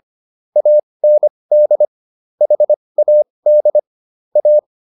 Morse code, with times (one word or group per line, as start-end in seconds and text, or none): 0.6-1.9 AND
2.4-3.8 HAD
4.4-4.6 A